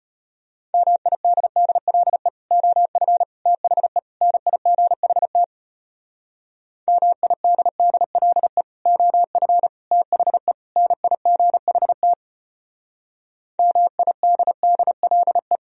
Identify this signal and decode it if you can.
Morse code, heard as a keyed tone, about 32 wpm, 700 Hz. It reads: MIDDLE OF THE NIGHT MIDDLE OF THE NIGHT MIDDLE